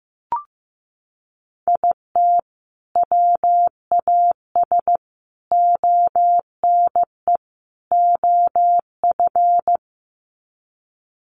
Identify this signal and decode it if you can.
Morse code, heard as a keyed tone, about 15 wpm, 700 Hz. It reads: IT WAS ONE OF
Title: Sets of 4 Words 15wpm